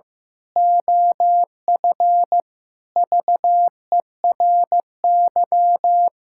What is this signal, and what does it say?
Morse code, heard as a keyed tone, about 16 wpm, 700 Hz. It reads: OF VERY
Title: Pangram Challenge v2 - 15wpm